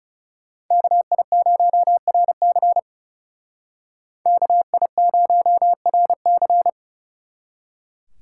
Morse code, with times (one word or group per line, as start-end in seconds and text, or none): 0.7-2.8 KI0RC
4.3-6.7 KI0RC